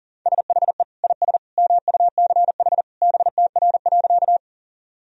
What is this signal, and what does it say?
Morse code, heard as a keyed tone, about 40 wpm, 700 Hz. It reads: SHE IS MUCH BTR.
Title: Sentences from Top 400 Words and Abbreviations 40wpm